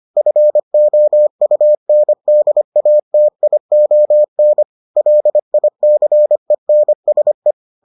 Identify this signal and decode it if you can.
Morse code, heard as a keyed tone, about 25 wpm, 600 Hz. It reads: FOUNDATION LICENSE